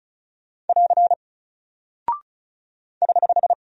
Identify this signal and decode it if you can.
Morse code, heard as a keyed tone, about 36 wpm, 700 Hz. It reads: <AR> <HH>
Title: Single Character (Letter:Number:Punctuation:Prosign) Alt 35wpm